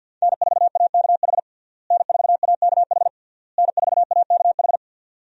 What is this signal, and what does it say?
Morse code, heard as a keyed tone, about 50 wpm, 700 Hz. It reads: N4AKH N4AKH N4AKH
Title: USA Call Signs - 5 Characters - Repeated 3x - 50wpm